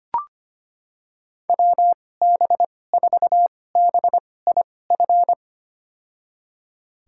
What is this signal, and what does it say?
Morse code, heard as a keyed tone, about 25 wpm, spaced as for 18 wpm, 700 Hz. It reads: WB4BIF
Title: USA Call Signs - 6 Characters - 2x Character Spacing - 25wpm